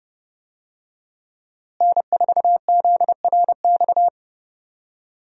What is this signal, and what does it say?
Morse code, heard as a keyed tone, about 30 wpm, 700 Hz. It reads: N4ZRX
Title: USA Call Signs - 5 Characters - Repeated 3x - 30wpm